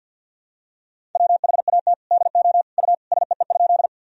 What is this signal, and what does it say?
Morse code, heard as a keyed tone, about 52 wpm, 700 Hz. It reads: WHAT DO U SEE?
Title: Sentences from Top 100 Words and Abbreviations 50wpm